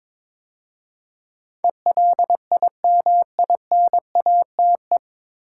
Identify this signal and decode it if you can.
Morse code, heard as a keyed tone, about 22 wpm, 700 Hz. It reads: ELIMINATE
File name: 9 Letter Words - First 3 Letters Given - 22wpm